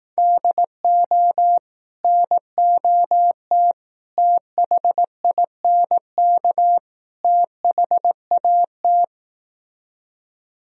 Morse code, heard as a keyed tone, about 18 wpm, 700 Hz. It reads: DO NOT THINK THAT
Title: Speed-Racing - Sets of 4 Words 15wpm